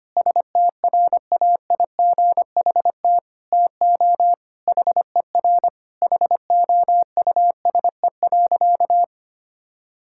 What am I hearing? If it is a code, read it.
Morse code, heard as a keyed tone, about 25 wpm, 700 Hz. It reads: STRAIGHT TO HER HOUSE.
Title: Sentences from Top 1000 Words - Review Part 6 - 25wpm